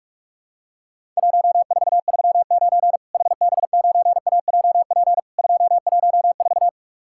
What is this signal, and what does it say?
Morse code, heard as a keyed tone, about 48 wpm, 700 Hz. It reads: 1439 HB9AJP 214
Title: Worked All Europe DX Contest - QTC Exchange - 45wpm